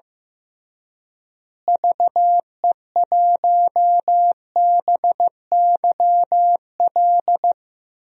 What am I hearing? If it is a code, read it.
Morse code, heard as a keyed tone, about 15 wpm, 700 Hz. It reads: VE1BYL
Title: DXCC-100 Call Signs - 6 Characters - Repeated 3x - 15wpm